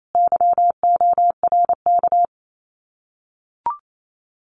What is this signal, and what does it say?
Morse code, heard as a keyed tone, about 28 wpm, 700 Hz. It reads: YORK
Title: Common QSO Phrases Encore 28wpm